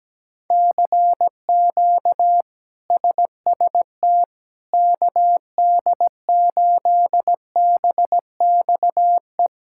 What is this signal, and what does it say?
Morse code, heard as a keyed tone, about 17 wpm, 700 Hz. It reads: CQ SST KD8BXE
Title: K1USN SST QSO Practice - 17wpm